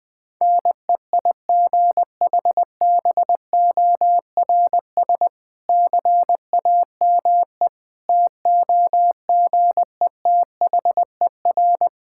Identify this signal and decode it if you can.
Morse code, heard as a keyed tone, about 21 wpm, 700 Hz. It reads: NEIGHBORS CAME TOGETHER
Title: Difficult Sentences with Hints - 20wpm